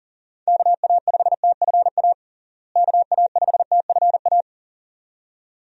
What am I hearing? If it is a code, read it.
Morse code, heard as a keyed tone, about 40 wpm, 700 Hz. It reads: KA5TFA KA5TFA
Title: USA Call Signs - 6 Characters - Repeated 2x - 40wpm